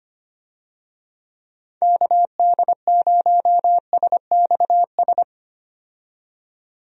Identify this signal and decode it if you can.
Morse code, heard as a keyed tone, about 25 wpm, 700 Hz. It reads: KD0SXS
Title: USA Call Signs - 6 Characters - 25wpm